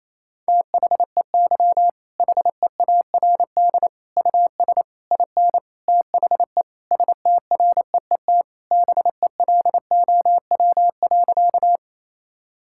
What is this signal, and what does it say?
Morse code, heard as a keyed tone, about 28 wpm, 700 Hz. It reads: THEY HEARD US IN THE STREET BELOW.
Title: Sentences from Top 1000 Words - Review Part 5 - 28wpm